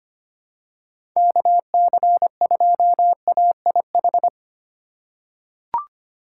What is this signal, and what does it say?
Morse code, heard as a keyed tone, about 25 wpm, 700 Hz. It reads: KC2AIH
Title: USA Call Signs - 6 Characters - 2x Character Spacing - 25wpm